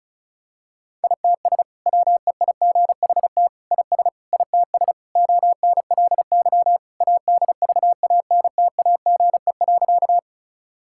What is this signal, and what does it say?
Morse code, heard as a keyed tone, about 34 wpm, 700 Hz. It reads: ITS WEIGHT IS ITS ONLY ADVANTAGE.
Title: Sentences from Top 1000 Words - Review Part 6 - 35wpm